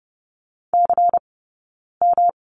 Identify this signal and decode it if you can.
Morse code, heard as a keyed tone, about 30 wpm, 700 Hz. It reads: C M
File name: Lesson 19d - Letter w - Two Characters 30wpm